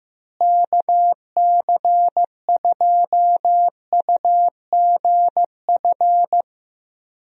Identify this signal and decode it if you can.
Morse code, heard as a keyed tone, about 15 wpm, 700 Hz. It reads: KC2UGF
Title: USA Call Signs - 6 Characters - Repeated 2x - 15wpm